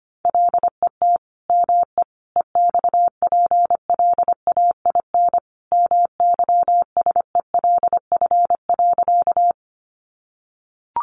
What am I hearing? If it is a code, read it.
Morse code, heard as a keyed tone, about 25 wpm, 700 Hz. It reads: LET ME EXPLAIN MYSELF.
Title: Sentences from Top 1300 Words 25wpm